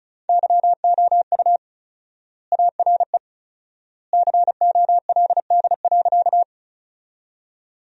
Morse code, heard as a keyed tone, about 34 wpm, 700 Hz. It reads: YOU ARE COLD.
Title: 4x Word-Spacing - Sentences from Top 400 Words - 35wpm